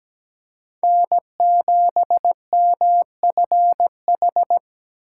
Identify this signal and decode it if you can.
Morse code, heard as a keyed tone, about 17 wpm, 700 Hz. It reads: N7MFH
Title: USA Callsign DE Callsign 17wpm